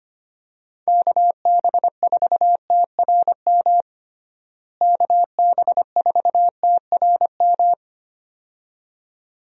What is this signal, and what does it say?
Morse code, heard as a keyed tone, about 25 wpm, 700 Hz. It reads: KB4TRM KB4TRM
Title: USA Call Signs - 6 Characters - Repeated 2x - 25wpm